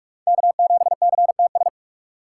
Morse code, heard as a keyed tone, about 46 wpm, 700 Hz. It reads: K7CTS
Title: USA Call Signs - 5 Characters - 2x Character Spacing - 45wpm